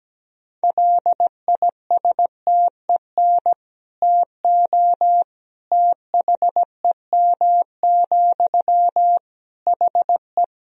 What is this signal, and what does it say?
Morse code, heard as a keyed tone, about 17 wpm, 700 Hz. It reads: LISTEN TO THEM, HE
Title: Sentences from Top 400 Words and Abbreviations 17wpm